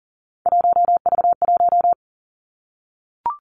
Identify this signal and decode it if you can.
Morse code, heard as a keyed tone, about 39 wpm, 700 Hz. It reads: 1V1
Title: Three Challenging Letter-Number Combinations 40wpm